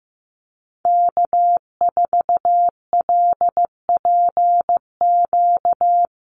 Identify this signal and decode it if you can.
Morse code, heard as a keyed tone, about 15 wpm, 700 Hz. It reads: K4LPQ
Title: NA SOTA Activator Callsigns 15wpm